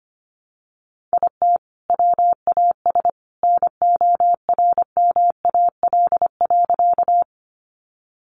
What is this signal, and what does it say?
Morse code, heard as a keyed tone, about 25 wpm, 700 Hz. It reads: IT WAS NORMAL.
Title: Sentences from Top 1200 Words 25wpm